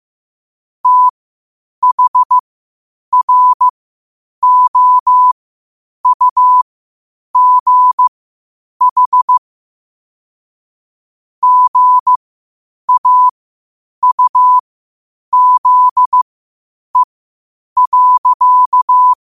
Morse code, heard as a keyed tone, about 15 wpm, spaced as for 8 wpm, 1000 Hz. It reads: THROUGH GAUZE.
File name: qotdfiles / 8wpm